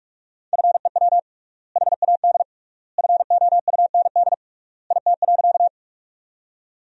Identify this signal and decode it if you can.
Morse code, heard as a keyed tone, about 46 wpm, 700 Hz. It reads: FEW HAD FOUND IT.
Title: 3x Word-Spacing - Sentences from Top 300 Words - 45wpm